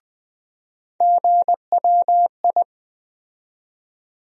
Morse code, heard as a keyed tone, about 20 wpm, 700 Hz. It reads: GWI
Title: USA Call Sign Postfixes - 3 Letters 20wpm